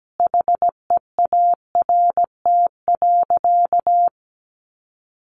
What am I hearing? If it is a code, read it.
Morse code, heard as a keyed tone, about 17 wpm, 700 Hz. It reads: HEART.
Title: Sentences from Top 1200 Words 17wpm